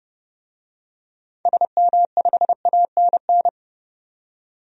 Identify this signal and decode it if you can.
Morse code, heard as a keyed tone, about 30 wpm, 700 Hz. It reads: SM5ANN
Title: DXCC-100 Call Signs - 6 Characters - 30wpm